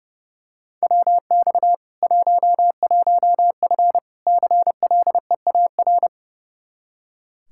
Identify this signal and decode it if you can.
Morse code, heard as a keyed tone, about 30 wpm, 700 Hz. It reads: WX 11F CLEAR
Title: Practice QSO - Exchange 2 - Info - 30wpm